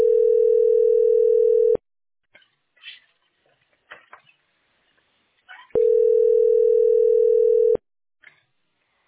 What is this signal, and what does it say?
Ringback tone (North America), 440 + 480 Hz.